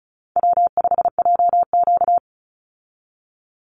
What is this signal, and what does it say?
Morse code, heard as a keyed tone, about 34 wpm, 700 Hz. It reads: W5JQ
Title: Call Signs Repeated 2x - 35wpm